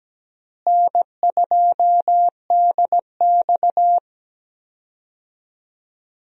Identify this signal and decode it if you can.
Morse code, heard as a keyed tone, about 17 wpm, 700 Hz. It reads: N2DX
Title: USA Call Signs - 4 Characters - Repeated 3x - 17wpm